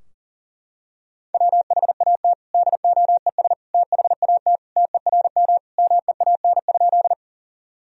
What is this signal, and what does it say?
Morse code, heard as a keyed tone, about 37 wpm, 700 Hz. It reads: WHAT DOES THAT TERM MEAN?
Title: Sentences from Top 1000 Words - Review Part 4 - 40wpm